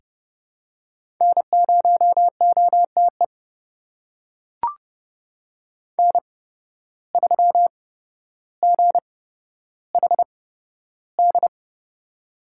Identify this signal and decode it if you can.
Morse code, heard as a keyed tone, about 32 wpm, spaced as for 13 wpm, 700 Hz. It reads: N0OTE N3GHD
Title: USA Call Signs - 5 Characters - 8x Character Spacing - 30wpm